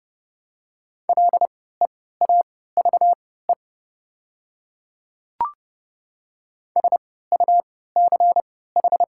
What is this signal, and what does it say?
Morse code, heard as a keyed tone, about 30 wpm, spaced as for 17 wpm, 700 Hz. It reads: LEAVE SUCH
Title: 3x Character Spacing - Top 500 Words - 30wpm